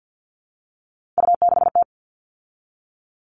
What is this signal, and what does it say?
Morse code, heard as a keyed tone, about 48 wpm, 700 Hz. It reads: U6T
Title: All Cut Numbers - 3-Digit 50wpm